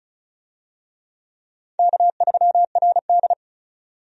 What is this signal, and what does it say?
Morse code, heard as a keyed tone, about 34 wpm, 700 Hz. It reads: K3RD
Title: USA Call Signs - 4 Characters - 35wpm